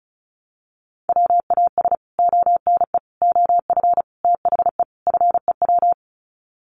Morse code, heard as a keyed tone, about 34 wpm, 700 Hz. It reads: WAS ONE OF THE FEW
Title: Sets of 5 Words Encore 35wpm